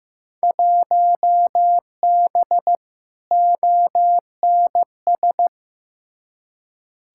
Morse code, heard as a keyed tone, about 15 wpm, 700 Hz. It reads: 1B ONS
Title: Field Day Exchanges Encore - 15wpm